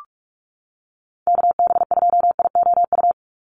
Morse code, heard as a keyed tone, about 44 wpm, 700 Hz. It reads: KB2IOU
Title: USA Call Signs 45wpm